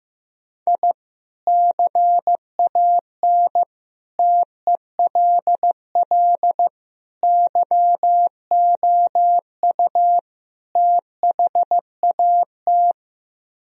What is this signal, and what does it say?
Morse code, heard as a keyed tone, about 16 wpm, 700 Hz. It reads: I CAN TELL YOU THAT
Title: Speed-Racing - Sets of 5 Words 15wpm